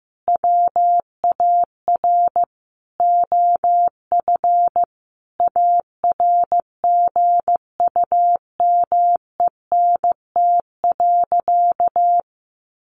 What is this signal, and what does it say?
Morse code, heard as a keyed tone, about 15 wpm, 700 Hz. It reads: WAR OF ARGUMENT.
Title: Sentences from Top 1400 Words 15wpm